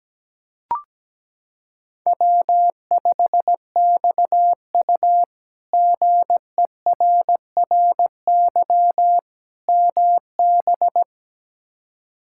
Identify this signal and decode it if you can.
Morse code, heard as a keyed tone, about 17 wpm, 700 Hz. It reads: W5XU GERRY MB
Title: K1USN SST QSO Practice - 17wpm